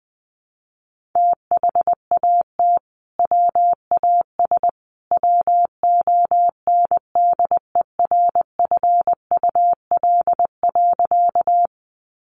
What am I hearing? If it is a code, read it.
Morse code, heard as a keyed tone, about 20 wpm, 700 Hz. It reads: THAT WAS WONDERFUL.
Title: Sentences from Top 1100 Words 20wpm